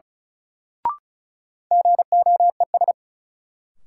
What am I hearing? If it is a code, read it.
Morse code, heard as a keyed tone, about 34 wpm, 700 Hz. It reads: GOES
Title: Sentences from Top 1000 Words - Review Part 5 - New Word Drill - 35wpm